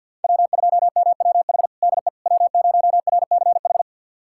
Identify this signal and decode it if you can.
Morse code, heard as a keyed tone, about 52 wpm, 700 Hz. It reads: W2MWH DE W0RCH
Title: USA Callsign DE Callsign - 5 Characters - 50wpm